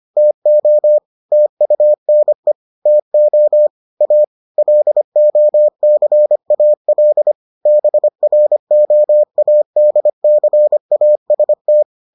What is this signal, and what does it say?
Morse code, heard as a keyed tone, about 25 wpm, 600 Hz. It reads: TO TUNE TO A LOCAL BROADCAST